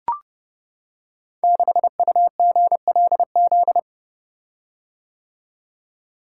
Morse code, heard as a keyed tone, about 30 wpm, 700 Hz. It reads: 6UGLZ
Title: Code Groups - 5 Letter:Numbers - 30wpm